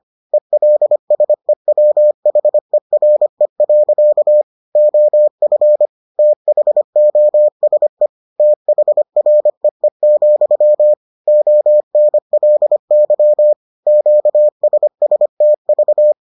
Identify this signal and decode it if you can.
Morse code, heard as a keyed tone, about 25 wpm, 600 Hz. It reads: ELSEWHERE. OF THOSE THREE, ONLY QSSTV